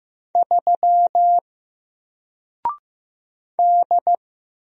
Morse code, heard as a keyed tone, about 15 wpm, 700 Hz. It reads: 3 D
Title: Single Letter-Number Alt 15wpm